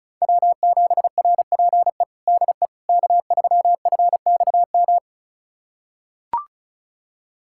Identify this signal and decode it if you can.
Morse code, heard as a keyed tone, about 36 wpm, 700 Hz. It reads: W7RPE DE K3FXM
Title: USA Callsign DE Callsign - 5 Characters - 35wpm